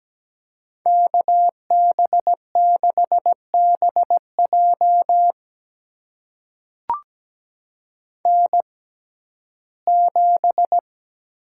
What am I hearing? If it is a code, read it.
Morse code, heard as a keyed tone, about 17 wpm, spaced as for 12 wpm, 700 Hz. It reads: KB6BJ N7
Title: USA Call Signs - 6x Character Spacing - 17wpm